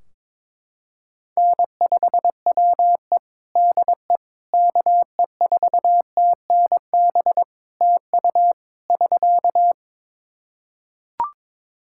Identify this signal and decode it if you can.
Morse code, heard as a keyed tone, about 22 wpm, 700 Hz. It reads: N5WE DE KE4TNB TU <SK>
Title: K1USN SST QSO Practice 22wpm